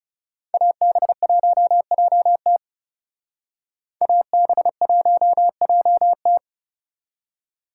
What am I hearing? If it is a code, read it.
Morse code, heard as a keyed tone, about 36 wpm, 700 Hz. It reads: AB1JT AB1JT
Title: Speed Racing - USA 2x2 Call Signs 30wpm